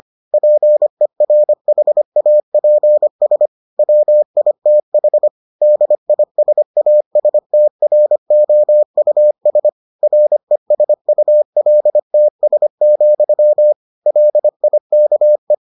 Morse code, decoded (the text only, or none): PERHAPS WITH DISASTROUS RESULTS, LIKE